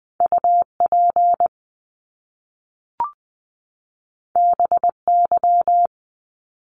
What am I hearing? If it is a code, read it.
Morse code, heard as a keyed tone, about 20 wpm, 700 Hz. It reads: UP BY
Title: Top Two Letter Combinations in Words 20wpm